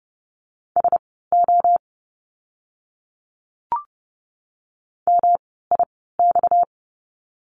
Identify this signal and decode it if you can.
Morse code, heard as a keyed tone, about 30 wpm, spaced as for 17 wpm, 700 Hz. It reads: SO MIX
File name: course / 30:10wpm / 230 Lesson 37f - Letter x - Words 30wpm